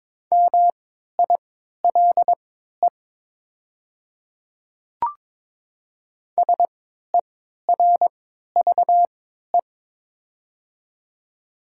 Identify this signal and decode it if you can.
Morse code, heard as a keyed tone, about 22 wpm, spaced as for 13 wpm, 700 Hz. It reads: MILE SERVE
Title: 3x Character Spacing - Top 500 Words - 22wpm